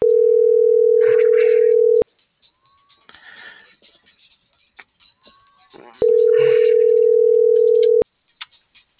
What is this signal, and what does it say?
Ringback tone (North America), 440 + 480 Hz.